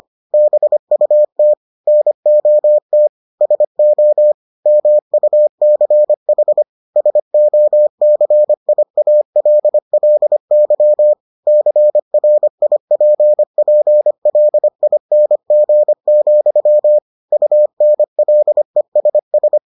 Morse code, heard as a keyed tone, about 25 wpm, 600 Hz. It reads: BUT NOT SO MUCH SOCIALLY CRIPPLING, UNLESS